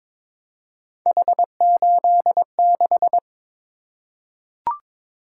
Morse code, heard as a keyed tone, about 22 wpm, 700 Hz. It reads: H86
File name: practice / 22wpm / Three Letter-Number Combinations - 22wpm